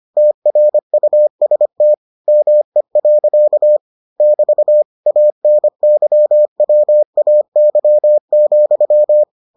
Morse code, heard as a keyed tone, about 25 wpm, 600 Hz. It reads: TRUST ME. <BT> ANYWAY,